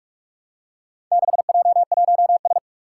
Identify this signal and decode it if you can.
Morse code, heard as a keyed tone, about 44 wpm, 700 Hz. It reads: 6J1S
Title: Four Challenging Letter-Number Combinations 45wpm